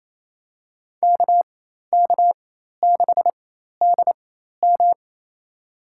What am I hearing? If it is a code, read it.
Morse code, heard as a keyed tone, about 28 wpm, spaced as for 13 wpm, 700 Hz. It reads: KK6DM
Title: USA Call Signs - 5 Characters - 4x Character Spacing - 28wpm